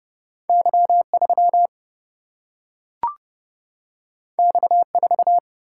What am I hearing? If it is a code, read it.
Morse code, heard as a keyed tone, about 30 wpm, 700 Hz. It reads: Y3 X4
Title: Single-Letter Single-Number 30wpm